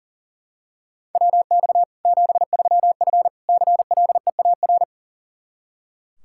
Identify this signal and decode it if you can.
Morse code, heard as a keyed tone, about 40 wpm, 700 Hz. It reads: WX 73F CLEAR
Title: Practice QSO - Exchange 2 - Info - 40wpm